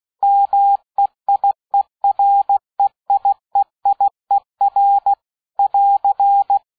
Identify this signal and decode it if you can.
Morse code, heard as a keyed tone, about 16 wpm, 800 Hz. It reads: MEIEREIEIER <AR>
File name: dk5ke / Fingerbrecher 75BpM